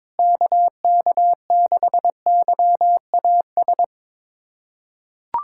Morse code, heard as a keyed tone, about 22 wpm, 700 Hz. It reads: KK6YAS
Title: USA Callsign DE Callsign - 6 Characters - 22wpm